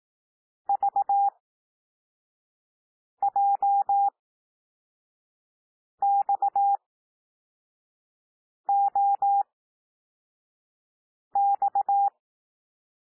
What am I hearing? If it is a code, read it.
Morse code, heard as a keyed tone, about 18 wpm, spaced as for 4 wpm, 800 Hz. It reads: VJXOX
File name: cwmp3 / L2 SOUND LETTERS 18 wpm